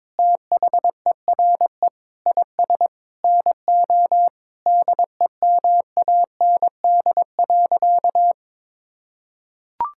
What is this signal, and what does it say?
Morse code, heard as a keyed tone, about 22 wpm, 700 Hz. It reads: THERE IS NO DEMAND.
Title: Sentences from Top 1000 Words - Review Part 5 - 22wpm